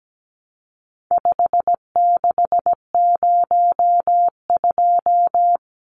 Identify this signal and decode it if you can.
Morse code, heard as a keyed tone, about 17 wpm, 700 Hz. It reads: 5602
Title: Numbers - 4-Digit 17wpm